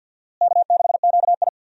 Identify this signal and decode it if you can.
Morse code, heard as a keyed tone, about 52 wpm, 700 Hz. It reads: K6QI
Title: USA Call Signs - 4 Characters - Repeated 3x - 50wpm